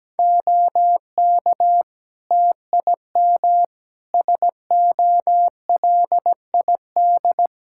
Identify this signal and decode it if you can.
Morse code, heard as a keyed tone, about 17 wpm, 700 Hz. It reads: OK TIM SOLID
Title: Practice QSO - Exchange 3 - More Info - 17wpm